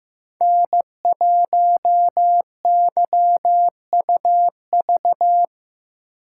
Morse code, heard as a keyed tone, about 15 wpm, 700 Hz. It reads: N1YUV
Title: USA Call Signs - 2x Character Spacing - 15wpm